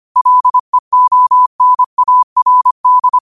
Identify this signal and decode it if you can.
Morse code, heard as a keyed tone, about 25 wpm, 1000 Hz. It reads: LEONARD
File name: qotdfiles / 25wpm